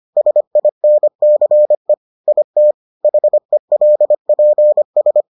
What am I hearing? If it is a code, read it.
Morse code, heard as a keyed tone, about 25 wpm, 600 Hz. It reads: SINCE IT HELPS